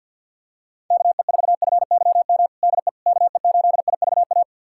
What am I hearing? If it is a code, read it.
Morse code, heard as a keyed tone, about 52 wpm, 700 Hz. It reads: KE4FYM DE KE8IVA
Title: USA Callsign DE Callsign - 6 Characters - 50wpm